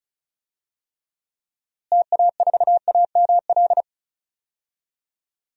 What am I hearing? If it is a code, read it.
Morse code, heard as a keyed tone, about 34 wpm, 700 Hz. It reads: TA4AML
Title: DXCC-100 Call Signs - 6 Characters - Repeated 3x - 35wpm